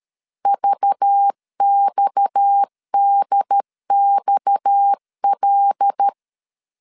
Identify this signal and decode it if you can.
Morse code, heard as a keyed tone, about 13 wpm, 800 Hz. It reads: VXDXL
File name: LEZIONI / 6 PKX0 / 6 PKX 12wpm 2024